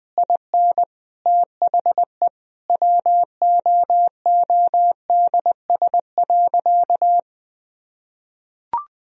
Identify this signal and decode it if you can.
Morse code, heard as a keyed tone, about 19 wpm, 700 Hz. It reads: IN THE WOODS.
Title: Sentences from Top 1000 Words - Review Part 2 - 20wpm